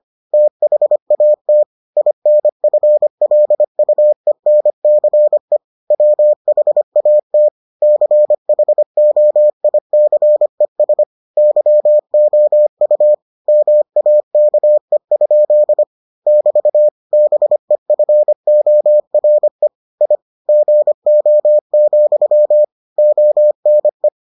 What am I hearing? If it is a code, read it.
Morse code, heard as a keyed tone, about 25 wpm, 600 Hz. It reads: THAT INFLUENCE WHAT CHOICES YOU MAKE? <BT> BEFORE I GO, ONE